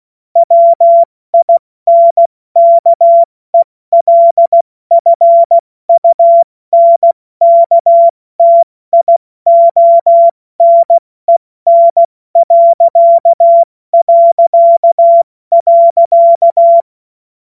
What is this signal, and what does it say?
Morse code, heard as a keyed tone, about 16 wpm, 680 Hz. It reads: WINKELFUNKTIONEN...
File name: Morsekurs / Witze / Witz 10.Oktober 2023 70 F80